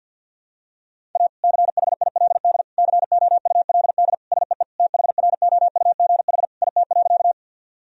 Morse code, heard as a keyed tone, about 51 wpm, 700 Hz. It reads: A CHILD COULD SEE THROUGH IT.
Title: Sentences from Top 1000 Words - Review Part 1 - 50wpm